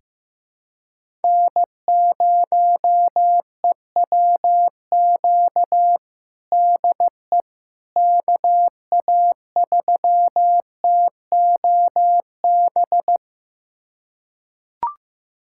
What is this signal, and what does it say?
Morse code, heard as a keyed tone, about 15 wpm, 700 Hz. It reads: N0EWQ DE KA3TOB
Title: USA Callsign DE Callsign 15wpm